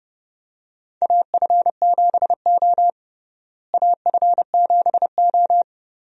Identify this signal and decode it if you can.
Morse code, heard as a keyed tone, about 30 wpm, 700 Hz. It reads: AF7O AF7O
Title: USA Call Signs - 4 Characters - Repeated 2x - 30wpm